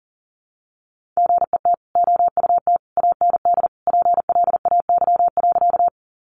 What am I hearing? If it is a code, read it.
Morse code, heard as a keyed tone, about 40 wpm, 700 Hz. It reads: GET OUT AND PLAY.
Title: Sentences from Top 200 Words 40wpm